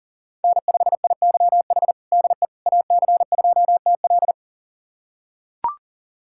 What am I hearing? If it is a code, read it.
Morse code, heard as a keyed tone, about 40 wpm, 700 Hz. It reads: N5IYH DE AC2TL
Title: USA Callsign DE Callsign - 5 Characters - 40wpm